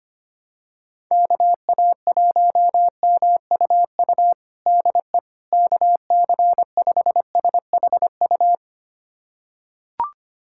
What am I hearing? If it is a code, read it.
Morse code, heard as a keyed tone, about 25 wpm, 700 Hz. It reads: KA1MUU DE KC5SHU
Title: USA Callsign DE Callsign - 6 Characters - 25wpm